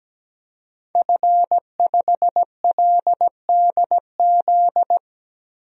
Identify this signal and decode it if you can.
Morse code, heard as a keyed tone, about 17 wpm, 700 Hz. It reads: F5LDZ
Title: DXCC-100 Call Signs - 5 Characters - 17wpm